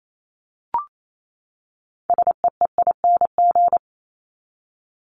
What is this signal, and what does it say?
Morse code, heard as a keyed tone, about 28 wpm, 700 Hz. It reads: SEEING
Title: ING Suffix - Top 100 Words with Base Words 28wpm